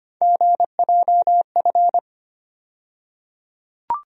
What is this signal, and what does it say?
Morse code, heard as a keyed tone, about 25 wpm, 700 Hz. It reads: GJF
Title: USA Call Sign Postfixes - 3 Letters 25wpm